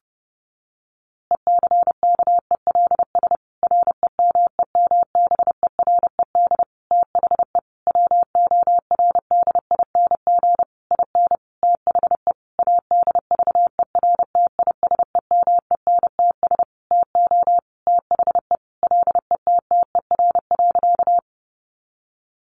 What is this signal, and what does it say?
Morse code, heard as a keyed tone, about 30 wpm, 700 Hz. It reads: ECKELS REMEMBERED THE WORDING IN THE ADVERTISEMENTS TO THE LETTER.
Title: A Sound Of Thunder 30wpm